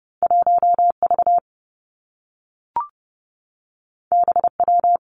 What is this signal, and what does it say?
Morse code, heard as a keyed tone, about 30 wpm, 700 Hz. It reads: 1 V B W
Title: Two Challenging Letter-Number Combinations 30wpm